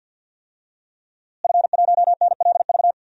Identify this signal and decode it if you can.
Morse code, heard as a keyed tone, about 52 wpm, 700 Hz. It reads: F1NLV
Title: DXCC-100 Call Signs - 5 Characters - Repeated 3x - 50wpm